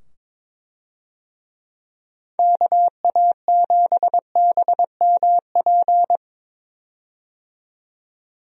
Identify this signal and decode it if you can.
Morse code, heard as a keyed tone, about 22 wpm, 700 Hz. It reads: KA7BMP
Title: USA Call Signs - 6 Characters - 22wpm